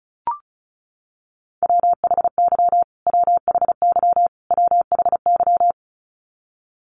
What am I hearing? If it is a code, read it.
Morse code, heard as a keyed tone, about 34 wpm, 700 Hz. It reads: WHY WHY WHY
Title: Top 500 Words Repeated 35wpm